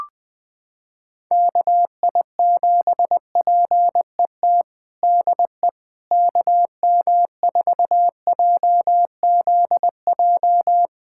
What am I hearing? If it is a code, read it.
Morse code, heard as a keyed tone, about 21 wpm, 700 Hz. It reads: KI7PET DE KM4JZJ
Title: USA Callsign DE Callsign - 6 Characters - 20wpm